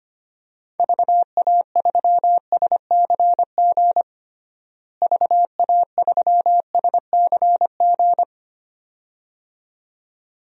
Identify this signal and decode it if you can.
Morse code, heard as a keyed tone, about 25 wpm, 700 Hz. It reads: VA3SCG VA3SCG
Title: DXCC-100 Call Signs - 6 Characters - Repeated 2x - 25wpm